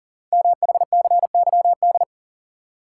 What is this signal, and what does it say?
Morse code, heard as a keyed tone, about 42 wpm, 690 Hz. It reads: MHCYD
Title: Code Groups - 5 Letters - 40wpm